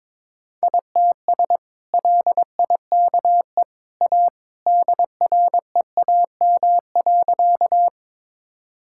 Morse code, heard as a keyed tone, about 21 wpm, 700 Hz. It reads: ITS LIKE A DREAM.
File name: practice / 22wpm / Sentences with 2-Word Phrases - 22wpm